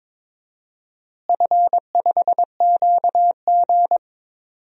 Morse code, heard as a keyed tone, about 22 wpm, 700 Hz. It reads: F5QG
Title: DXCC-100 Call Signs - 4 Characters - Repeated 2x - 22wpm